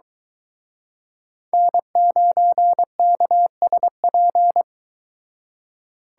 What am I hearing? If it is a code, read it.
Morse code, heard as a keyed tone, about 23 wpm, 700 Hz. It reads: N9KSP